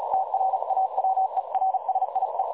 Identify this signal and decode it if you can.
Morse code, heard as a keyed tone, about 37 wpm, 760 Hz, with noise. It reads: TRAWERSIE
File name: dlazaawansowanych / nagraniazpasma / websdr recording 3565.0kHz